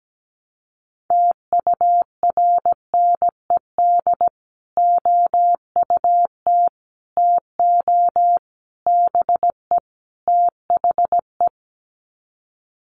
Morse code, heard as a keyed tone, about 17 wpm, 700 Hz. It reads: TURNED OUT TO BE THE
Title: Sets of 5 Words Encore 17wpm